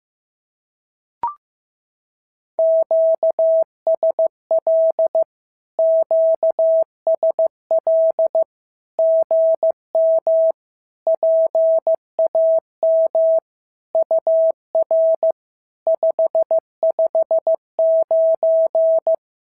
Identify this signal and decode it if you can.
Morse code, heard as a keyed tone, about 15 wpm, 650 Hz. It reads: QSL QSL GM PAM UR 559
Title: SOTA QSO Practice 15wpm